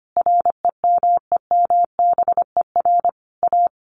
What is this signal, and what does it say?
Morse code, heard as a keyed tone, about 25 wpm, 700 Hz. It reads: REMEMBER A
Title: Sentences from Top 1200 Words 25wpm